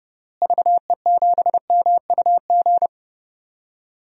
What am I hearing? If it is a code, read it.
Morse code, heard as a keyed tone, about 32 wpm, 700 Hz. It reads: VE7MUG
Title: DXCC-100 Call Signs - 6 Characters - Repeated 3x - 30wpm